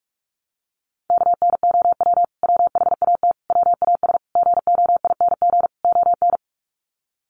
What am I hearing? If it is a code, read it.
Morse code, heard as a keyed tone, about 44 wpm, 700 Hz. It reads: KNOW WHAT WAS GOING ON
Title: Sets of 5 Words Encore2 45wpm